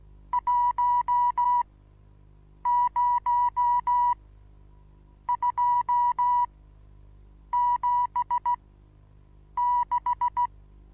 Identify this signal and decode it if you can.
Morse code, heard as a keyed tone, about 16 wpm, spaced as for 7 wpm, 970 Hz. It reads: 10276
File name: Onlinelehrgang / morsen / DJ4UF Morsen25